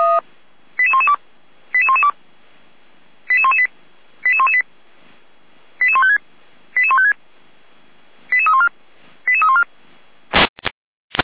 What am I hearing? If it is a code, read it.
Five-tone selective call (ZVEI), heard as quick runs of five tones, tones of about 80 ms. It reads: 88112 88112 88118 88118 88156 88156 88314 88314